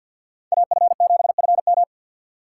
Morse code, heard as a keyed tone, about 47 wpm, 700 Hz. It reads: AF7FM
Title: USA Call Signs - 5 Characters - 4x Character Spacing - 50wpm